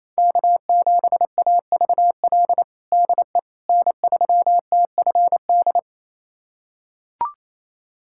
Morse code, heard as a keyed tone, about 28 wpm, 700 Hz. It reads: K7AVL DE N3TFD
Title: USA Callsign DE Callsign - 5 Characters - 28wpm